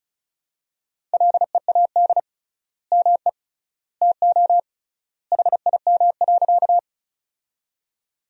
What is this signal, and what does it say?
Morse code, heard as a keyed tone, about 34 wpm, 700 Hz. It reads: LEAD ME TO HIM.
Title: 3x Word-Spacing - Sentences from Top 500 Words - 35wpm